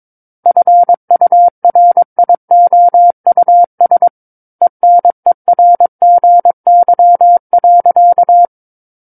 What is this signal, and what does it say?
Morse code, heard as a keyed tone, about 22 wpm, 700 Hz. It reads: FURIOUS ENERGY.
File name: Book One / 13-Chapter 13 - 22 WPM